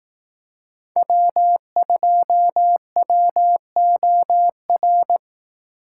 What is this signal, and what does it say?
Morse code, heard as a keyed tone, about 18 wpm, 700 Hz. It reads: W2WOR
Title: Speed Racing - USA 1x3 Call Signs 15wpm